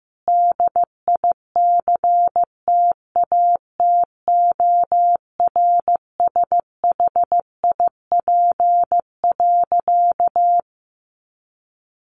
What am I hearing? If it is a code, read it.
Morse code, heard as a keyed tone, about 15 wpm, 700 Hz. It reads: DICTATORSHIP.
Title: A Sound Of Thunder 15wpm